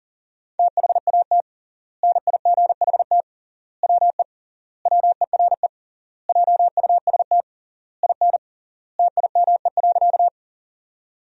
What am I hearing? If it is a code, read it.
Morse code, heard as a keyed tone, about 40 wpm, 700 Hz. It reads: THAT NIGHT WE WERE JUST IN TIME.
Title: 3x Word-Spacing - Sentences from Top 300 Words - 40wpm